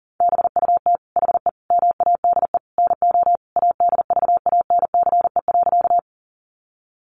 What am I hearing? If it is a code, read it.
Morse code, heard as a keyed tone, about 40 wpm, 700 Hz. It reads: BUT HE MADE NO ADVANCE.
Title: Sentences from Top 1400 Words 40wpm